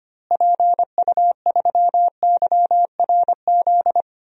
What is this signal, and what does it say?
Morse code, heard as a keyed tone, about 25 wpm, 700 Hz. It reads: PU3YRZ
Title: DXCC-100 Call Signs - 6 Characters - Repeated 3x - 25wpm